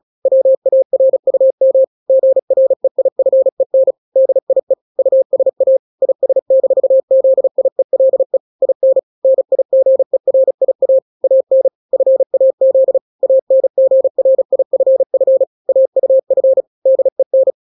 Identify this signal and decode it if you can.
Morse code, heard as a keyed tone, about 36 wpm, 530 Hz. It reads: WARUM GREIFEN DIE USA IS-ZIELE IN NIGERIA AN FAZ ANGRIFF AUF DEN
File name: daily / 35wpm